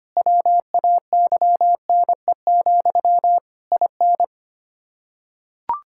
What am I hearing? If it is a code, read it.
Morse code, heard as a keyed tone, about 25 wpm, 700 Hz. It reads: WAYNE, IN
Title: Practice QSO - Exchange 1 - Beginning - 25wpm